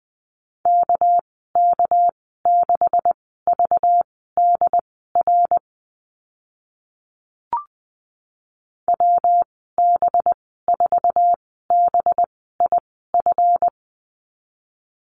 Morse code, heard as a keyed tone, about 20 wpm, spaced as for 14 wpm, 700 Hz. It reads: KK6VDR WB4BIF
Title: 231 Lesson 37g - Letter x - Call Signs 20wpm